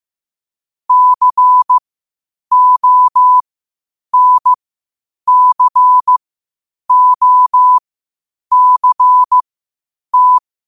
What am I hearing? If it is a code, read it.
Morse code, heard as a keyed tone, about 15 wpm, spaced as for 8 wpm, 1000 Hz. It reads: CONCOCT
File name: qotdfiles / 8wpm